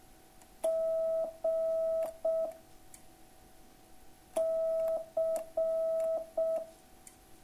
Morse code, heard as a keyed tone, about 6 wpm, 660 Hz. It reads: G C